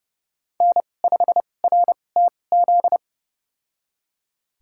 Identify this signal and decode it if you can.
Morse code, heard as a keyed tone, about 30 wpm, spaced as for 22 wpm, 700 Hz. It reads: N5RTZ
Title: USA Call Signs - 5 Characters - 2x Character Spacing - 30wpm